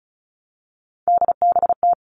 Morse code, heard as a keyed tone, about 34 wpm, 700 Hz. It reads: DBT
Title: All Cut Numbers - 3-Digit 35wpm